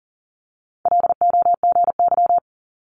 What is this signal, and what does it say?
Morse code, heard as a keyed tone, about 38 wpm, 700 Hz. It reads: LOGY
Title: Multi-letter Phonemes Quick Fire 40wpm